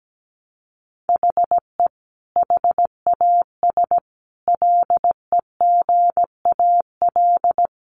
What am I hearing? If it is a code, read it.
Morse code, heard as a keyed tone, about 17 wpm, 700 Hz. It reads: HE HAS LEGAL
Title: Sentences from Top 1200 Words 17wpm